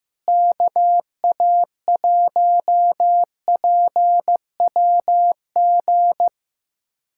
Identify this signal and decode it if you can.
Morse code, heard as a keyed tone, about 15 wpm, 700 Hz. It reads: KA1PWG
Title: USA Callsign DE Callsign 15wpm